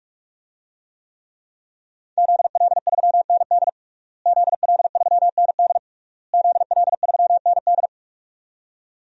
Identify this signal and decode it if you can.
Morse code, heard as a keyed tone, about 46 wpm, 700 Hz. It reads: ZL3ND ZL3ND ZL3ND
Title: DXCC-100 Call Signs - 5 Characters - Repeated 3x - 45wpm